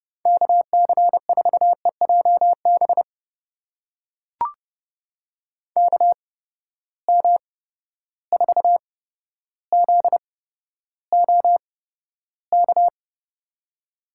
Morse code, heard as a keyed tone, about 30 wpm, spaced as for 13 wpm, 700 Hz. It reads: KC4EJB KM4ZOK
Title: USA Call Signs - 6 Characters - 8x Character Spacing - 30wpm